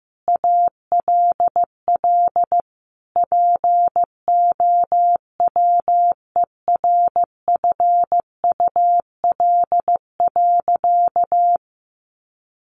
Morse code, heard as a keyed tone, about 15 wpm, 700 Hz. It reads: ALL POWERFUL.
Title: Sentences from Top 1300 Words 15wpm